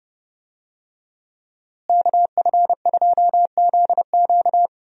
Morse code, heard as a keyed tone, about 32 wpm, 700 Hz. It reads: KF2ZQ
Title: USA Call Signs - 5 Characters - Repeated 3x - 30wpm